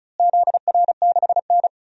Morse code, heard as a keyed tone, about 36 wpm, 700 Hz. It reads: ZR6N
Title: DXCC-100 Call Signs - 4 Characters - Repeated 3x - 35wpm